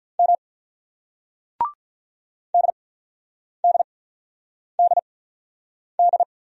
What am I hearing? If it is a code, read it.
Morse code, heard as a keyed tone, about 46 wpm, spaced as for 9 wpm, 700 Hz. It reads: M DDDD